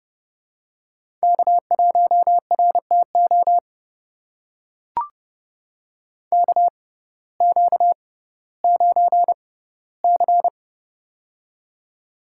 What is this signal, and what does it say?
Morse code, heard as a keyed tone, about 30 wpm, spaced as for 17 wpm, 700 Hz. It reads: K1RTO KQ9C
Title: USA Call Signs - 6x Character Spacing - 30wpm